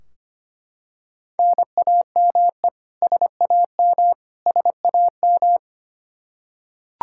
Morse code, heard as a keyed tone, about 25 wpm, 700 Hz. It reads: NAME SAM SAM
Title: Practice QSO - Exchange 1 - Beginning - 25wpm